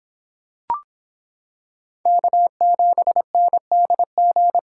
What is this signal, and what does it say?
Morse code, heard as a keyed tone, about 25 wpm, 700 Hz. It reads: K7NDG